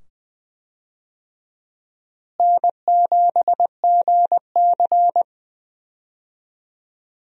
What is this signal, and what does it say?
Morse code, heard as a keyed tone, about 20 wpm, 700 Hz. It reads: N7GC
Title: USA Call Signs - 4 Characters - Repeated 2x - 20wpm